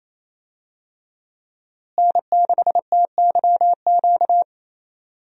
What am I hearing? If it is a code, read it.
Morse code, heard as a keyed tone, about 28 wpm, 700 Hz. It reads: N6TYQ
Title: USA Call Signs - 5 Characters - Repeated 2x - 28wpm